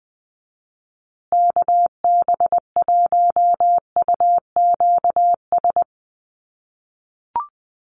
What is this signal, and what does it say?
Morse code, heard as a keyed tone, about 20 wpm, 700 Hz. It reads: KB1UQS
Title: USA Call Signs Repeated 3x 20wpm